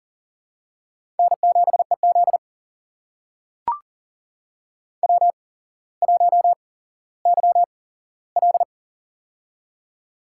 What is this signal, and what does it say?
Morse code, heard as a keyed tone, about 38 wpm, spaced as for 17 wpm, 690 Hz. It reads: N7EZ W1YL
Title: USA Call Signs - 4 Characters - 8x Character Spacing - 40wpm